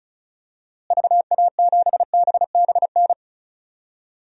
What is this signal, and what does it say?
Morse code, heard as a keyed tone, about 36 wpm, 700 Hz. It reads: VA7BBN
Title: DXCC-100 Call Signs - 6 Characters - Repeated 3x - 35wpm